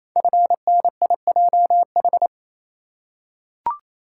Morse code, heard as a keyed tone, about 28 wpm, 700 Hz. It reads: FNIJH
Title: Code Groups - 5 Letters - 28wpm